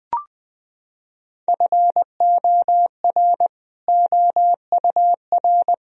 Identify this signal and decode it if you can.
Morse code, heard as a keyed tone, about 20 wpm, 700 Hz. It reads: FOR OUR
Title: Sentences from Top 1000 Words - Review Part 4 - 20wpm